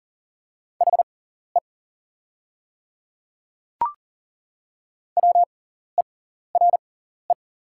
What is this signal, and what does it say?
Morse code, heard as a keyed tone, about 40 wpm, spaced as for 14 wpm, 700 Hz. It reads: HE WERE
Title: 6x Character Spacing - Top 200 Words - 40wpm